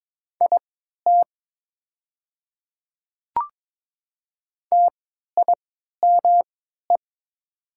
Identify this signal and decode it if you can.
Morse code, heard as a keyed tone, about 22 wpm, spaced as for 12 wpm, 700 Hz. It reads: IT TIME
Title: 3x Character Spacing - Top 100 Words - 22wpm